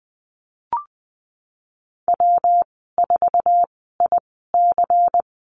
Morse code, heard as a keyed tone, about 20 wpm, spaced as for 15 wpm, 700 Hz. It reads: W4IC
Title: 128 Lesson 21g - Number 6 - Call Signs 20wpm